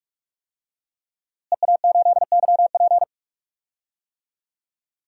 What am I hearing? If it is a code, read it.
Morse code, heard as a keyed tone, about 47 wpm, 700 Hz. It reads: EA8YP
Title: DXCC-100 Call Signs - 5 Characters - Repeated 3x - 45wpm